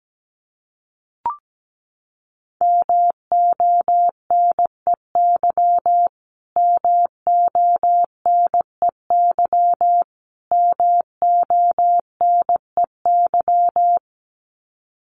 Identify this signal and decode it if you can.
Morse code, heard as a keyed tone, about 17 wpm, 700 Hz. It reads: MONEY MONEY MONEY
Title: Top 500 Words Repeated 17wpm